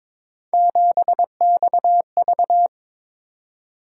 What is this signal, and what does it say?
Morse code, heard as a keyed tone, about 22 wpm, 700 Hz. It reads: 7XV
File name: practice / 22wpm / Three Letter-Number Combinations - 22wpm